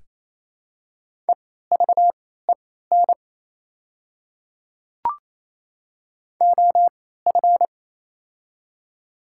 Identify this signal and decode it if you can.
Morse code, heard as a keyed tone, about 27 wpm, spaced as for 16 wpm, 690 Hz. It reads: EVEN OF
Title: 3x Character Spacing - Top 300 Words - 28wpm